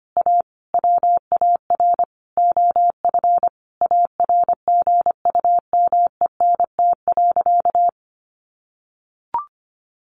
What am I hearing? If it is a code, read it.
Morse code, heard as a keyed tone, about 25 wpm, 700 Hz. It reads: A WAR OF ARGUMENT.
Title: Sentences from Top 1400 Words 25wpm